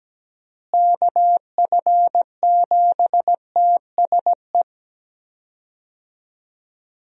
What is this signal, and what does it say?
Morse code, heard as a keyed tone, about 17 wpm, 700 Hz. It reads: KF7TSE
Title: USA Call Signs - 6 Characters - Repeated 3x - 17wpm